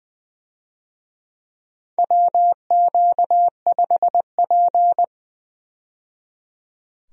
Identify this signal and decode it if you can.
Morse code, heard as a keyed tone, about 20 wpm, 700 Hz. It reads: WQ5P
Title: USA Call Signs - 4 Characters - 20wpm